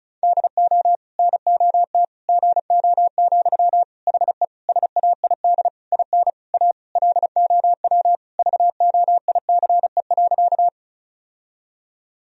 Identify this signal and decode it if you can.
Morse code, heard as a keyed tone, about 36 wpm, 700 Hz. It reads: DO NOT GO, HE SAID IN A LOW VOICE.
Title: Sentences from Top 1000 Words - Review Part 1 - 35wpm